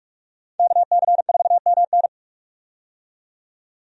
Morse code, heard as a keyed tone, about 46 wpm, 700 Hz. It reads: KC4MN
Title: USA Call Signs - 5 Characters - 45wpm